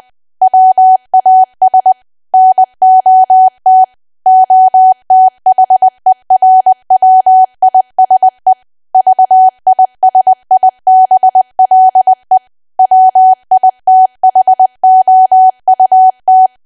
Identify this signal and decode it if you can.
Morse code, heard as a keyed tone, about 20 wpm, 750 Hz. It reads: WAS NOT OTHERWISE VISIBLE WITHOUT
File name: Archive / 20 WPM / 20WPM